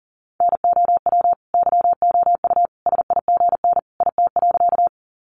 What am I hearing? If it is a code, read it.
Morse code, heard as a keyed tone, about 40 wpm, 700 Hz. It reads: NOW YOU SIGN IT.
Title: Sentences from Top 1000 Words 40wpm